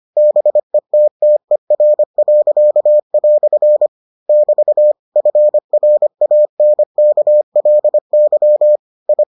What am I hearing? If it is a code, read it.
Morse code, heard as a keyed tone, about 25 wpm, 600 Hz. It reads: BETTER." <BT> FRANKLY I